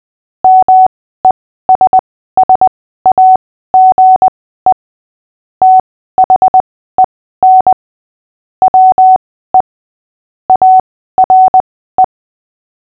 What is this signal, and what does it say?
Morse code, heard as a keyed tone, about 20 wpm, spaced as for 14 wpm, 750 Hz. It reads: MESSAGE THEN WE ARE
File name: cw / fists / Short-Story-8-14wpm